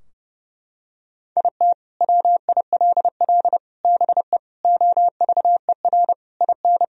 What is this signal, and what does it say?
Morse code, heard as a keyed tone, about 30 wpm, 700 Hz. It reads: IT WILL BE OVER IN
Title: Sentences from Top 1000 Words - Review Part 2 - 30wpm